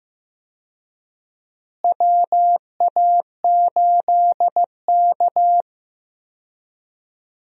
Morse code, heard as a keyed tone, about 15 wpm, 700 Hz. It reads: WA8K
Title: USA Call Signs - 4 Characters - Repeated 2x - 15wpm